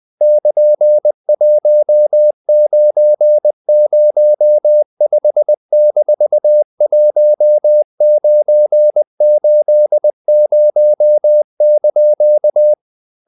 Morse code, heard as a keyed tone, about 20 wpm, 600 Hz. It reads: <KN>1905-1980)
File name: CW / Practice / EnglishQuotations0320